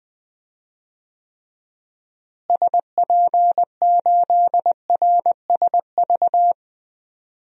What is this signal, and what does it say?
Morse code, heard as a keyed tone, about 20 wpm, 700 Hz. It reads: SP8RSV
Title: DXCC-100 Call Signs - 6 Characters - Repeated 2x - 20wpm